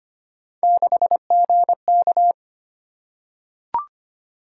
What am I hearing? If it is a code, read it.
Morse code, heard as a keyed tone, about 25 wpm, 700 Hz. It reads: 6GK
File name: Three Letter-Number Combinations - 25wpm